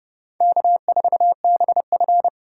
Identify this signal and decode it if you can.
Morse code, heard as a keyed tone, about 32 wpm, 700 Hz. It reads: K4BF